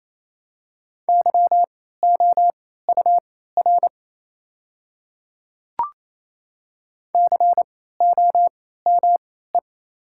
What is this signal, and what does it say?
Morse code, heard as a keyed tone, about 28 wpm, spaced as for 16 wpm, 690 Hz. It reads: YOUR COME
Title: 3x Character Spacing - Top 100 Words - 28wpm